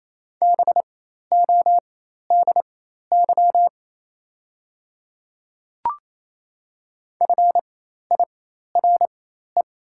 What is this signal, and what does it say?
Morse code, heard as a keyed tone, about 28 wpm, spaced as for 13 wpm, 700 Hz. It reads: BODY FIRE
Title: 4x Character Spacing - Top 500 Words - 28wpm